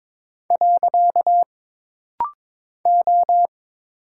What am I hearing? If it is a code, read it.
Morse code, heard as a keyed tone, about 22 wpm, 700 Hz. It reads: . O
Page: This rapid-fire timing is geared to push your ICR (Instant Character Recognition) skills.